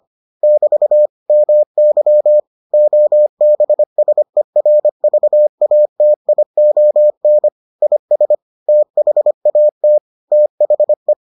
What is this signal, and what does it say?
Morse code, heard as a keyed tone, about 25 wpm, 600 Hz. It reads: <BT> MY OBSERVATION IS THAT THE